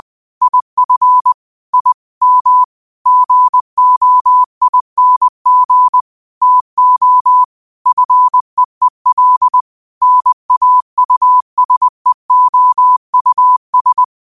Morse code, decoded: IF I M GOING TO FEEL NAUSEOUS

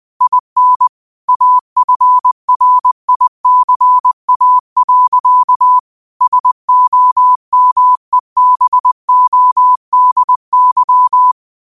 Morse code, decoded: IN AFRICA. SOMEBODY

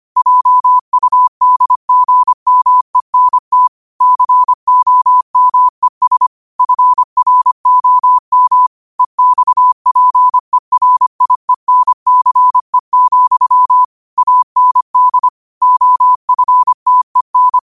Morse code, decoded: JUDGMENT COMES FROM EXPERIENCE, AND OFTEN

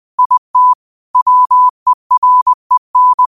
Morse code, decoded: IT WEREN

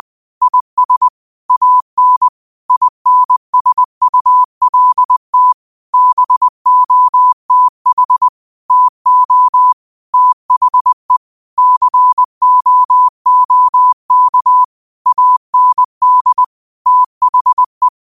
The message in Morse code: IS AN INSULT BOTH TO THE COOK AND THE

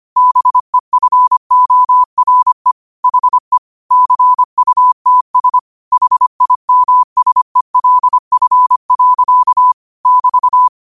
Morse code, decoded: BEFORE HE CUTS HIMSELF. <BT>

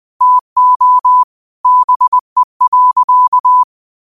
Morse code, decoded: TO BE.

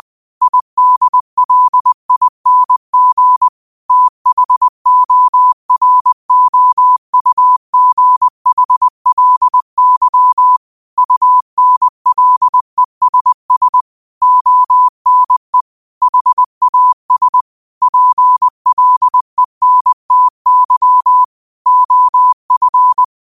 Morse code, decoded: IDLING THOROUGHLY UNLESS ONE HAS PLENTY OF

